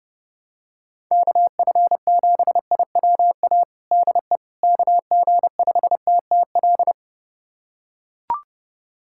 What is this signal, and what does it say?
Morse code, heard as a keyed tone, about 30 wpm, 700 Hz. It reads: KF7IWA DE KG5TTL
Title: USA Callsign DE Callsign 30wpm